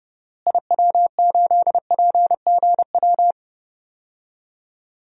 Morse code, heard as a keyed tone, about 30 wpm, 700 Hz. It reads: IW8PGW
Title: DXCC-100 Call Signs - 6 Characters - Repeated 2x - 30wpm